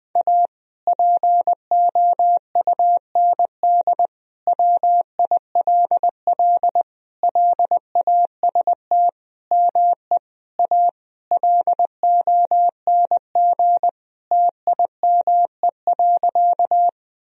Morse code, heard as a keyed tone, about 19 wpm, 700 Hz. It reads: A POUND WILL LAST ME A LONG TIME.
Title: Sentences from Top 1000 Words - Review Part 2 - 20wpm